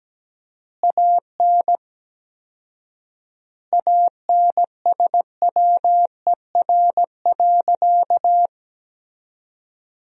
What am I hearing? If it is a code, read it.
Morse code, heard as a keyed tone, about 17 wpm, 700 Hz. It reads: AN ANSWER.
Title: 4x Word-Spacing - Sentences from Top 300 Words - 17wpm